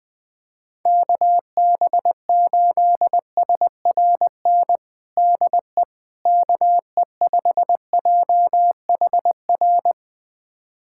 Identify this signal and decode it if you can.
Morse code, heard as a keyed tone, about 20 wpm, 700 Hz. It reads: KB8SRN DE KE5JHR
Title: USA Callsign DE Callsign - 6 Characters - 20wpm